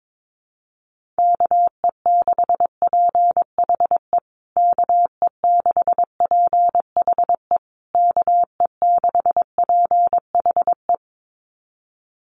Morse code, heard as a keyed tone, about 22 wpm, 700 Hz. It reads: KE6PHE KE6PHE KE6PHE
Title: USA Call Signs Repeated 3x 22wpm